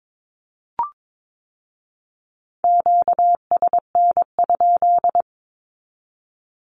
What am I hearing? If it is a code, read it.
Morse code, heard as a keyed tone, about 22 wpm, 700 Hz. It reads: QSN?
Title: Q-Codes with Spoken Meaning - 22wpm